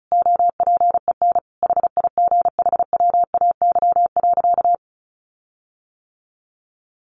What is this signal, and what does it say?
Morse code, heard as a keyed tone, about 34 wpm, 700 Hz. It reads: OPEN HIGHWAY.
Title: Sentences from Top 1400 Words 35wpm